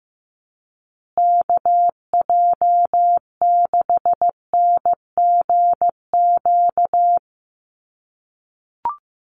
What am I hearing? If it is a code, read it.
Morse code, heard as a keyed tone, about 15 wpm, 700 Hz. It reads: KJ6NGQ
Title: USA Call Signs Repeated 2x 15wpm